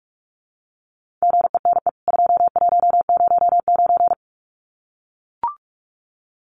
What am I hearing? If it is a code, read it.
Morse code, heard as a keyed tone, about 44 wpm, 700 Hz. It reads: GENE 2109
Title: CWT Exchange 45wpm